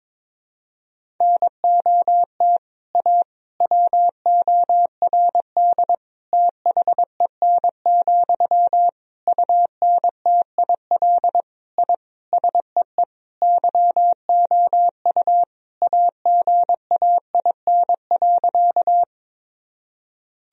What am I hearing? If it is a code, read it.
Morse code, heard as a keyed tone, about 22 wpm, 700 Hz. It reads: NOT A WORD THEN, UNTIL I SEE YOU AGAIN.
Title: Sentences from Top 1000 Words - Review Part 1 - 22wpm